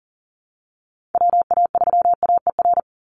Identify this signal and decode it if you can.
Morse code, heard as a keyed tone, about 40 wpm, 700 Hz. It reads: WA3AER
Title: Call Signs Repeated 2x - 40wpm